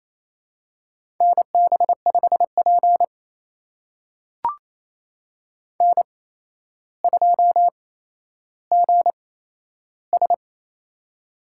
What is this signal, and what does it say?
Morse code, heard as a keyed tone, about 27 wpm, spaced as for 12 wpm, 700 Hz. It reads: NB5P N2GS
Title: USA Call Signs - 4 Characters - 8x Character Spacing - 28wpm